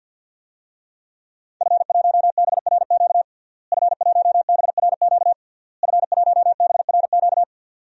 Morse code, heard as a keyed tone, about 50 wpm, 700 Hz. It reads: F1BRQ F1BRQ F1BRQ
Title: DXCC-100 Call Signs - 5 Characters - Repeated 3x - 50wpm